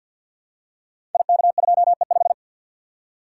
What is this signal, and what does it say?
Morse code, heard as a keyed tone, about 52 wpm, 690 Hz. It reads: IK2E5
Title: Code Groups - 5 Letter:Numbers - 50wpm